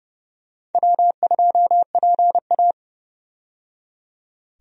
Morse code, heard as a keyed tone, about 30 wpm, 700 Hz. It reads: W2PA
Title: USA Call Signs - 4 Characters - 30wpm